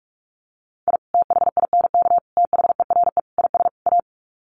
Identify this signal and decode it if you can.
Morse code, heard as a keyed tone, about 44 wpm, 700 Hz. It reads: I THINK THERE IS A
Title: Sets of 5 Words 45wpm